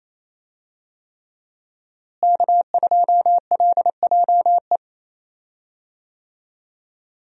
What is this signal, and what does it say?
Morse code, heard as a keyed tone, about 28 wpm, 700 Hz. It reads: K2LJE
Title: USA Call Signs - 5 Characters - Repeated 3x - 28wpm